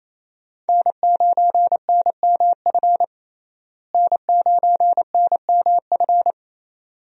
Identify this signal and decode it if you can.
Morse code, heard as a keyed tone, about 27 wpm, 700 Hz. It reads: N9NMF N9NMF
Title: USA Call Signs - 5 Characters - Repeated 2x - 28wpm